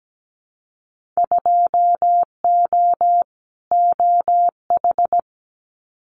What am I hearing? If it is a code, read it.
Morse code, heard as a keyed tone, about 17 wpm, 700 Hz. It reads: 2O OH
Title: Winter Field Day Exchanges 17wpm